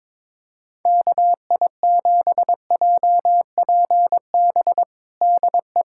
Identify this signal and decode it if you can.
Morse code, heard as a keyed tone, about 22 wpm, 700 Hz. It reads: KI7JPB DE
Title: USA Callsign DE Callsign 22wpm